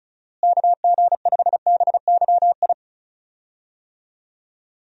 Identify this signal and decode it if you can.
Morse code, heard as a keyed tone, about 36 wpm, 700 Hz. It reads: KG5BYI
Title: USA Call Signs - 6 Characters - Repeated 3x - 35wpm